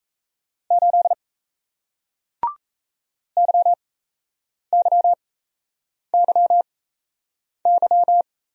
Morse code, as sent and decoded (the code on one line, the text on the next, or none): ---.. / -.-- -.-- -.-- -.--
8 YYYY